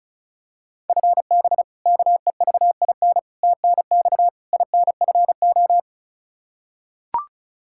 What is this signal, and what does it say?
Morse code, heard as a keyed tone, about 36 wpm, 700 Hz. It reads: FB KEVIN TNX INFO
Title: Practice QSO - Exchange 3 - More Info - 35wpm